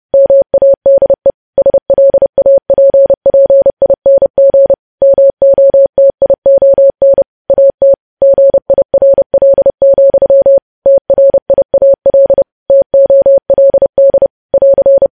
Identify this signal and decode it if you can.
Morse code, heard as a keyed tone, about 30 wpm, 550 Hz. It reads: MADE SLAPPING MOTION AT GIRL, TRIAL TOLD <AR>
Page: news headlines in morse code at 30 wpm